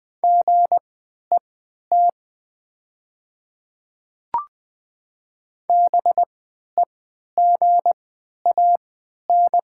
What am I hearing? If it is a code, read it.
Morse code, heard as a keyed tone, about 20 wpm, spaced as for 11 wpm, 700 Hz. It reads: GET BEGAN
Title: 3x Character Spacing - Top 500 Words - 20wpm